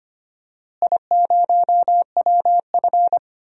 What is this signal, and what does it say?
Morse code, heard as a keyed tone, about 25 wpm, 700 Hz. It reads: I0WF
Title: DXCC-100 Call Signs - 4 Characters - Repeated 2x - 25wpm